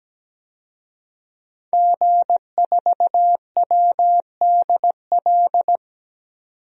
Morse code, heard as a keyed tone, about 17 wpm, 700 Hz. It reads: G4WDL
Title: DXCC-100 Call Signs - 5 Characters - 17wpm